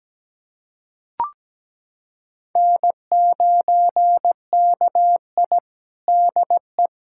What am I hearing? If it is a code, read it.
Morse code, heard as a keyed tone, about 17 wpm, 700 Hz. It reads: N9KI DE
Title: USA Callsign DE Callsign - 4 Characters - 17wpm